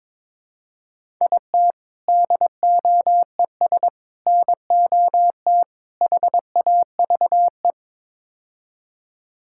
Morse code, heard as a keyed tone, about 21 wpm, 700 Hz. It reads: IT DOES NOT HAVE
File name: practice / 22wpm / Build A Phrase - 5-Word Phrases - 22wpm